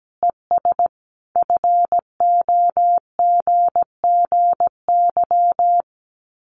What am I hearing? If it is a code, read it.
Morse code, heard as a keyed tone, about 17 wpm, 700 Hz. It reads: ES FOGGY
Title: Common QSO Phrases 17wpm